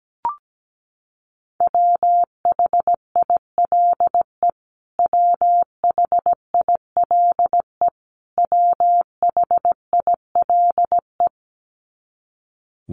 Morse code, heard as a keyed tone, about 17 wpm, 700 Hz. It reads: WHILE WHILE WHILE
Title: Top 500 Words Repeated 17wpm